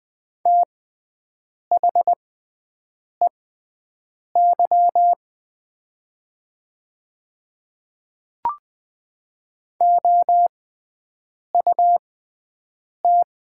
Morse code, heard as a keyed tone, about 20 wpm, spaced as for 7 wpm, 700 Hz. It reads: THEY OUT
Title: 6x Character Spacing - Top 100 Words - 20wpm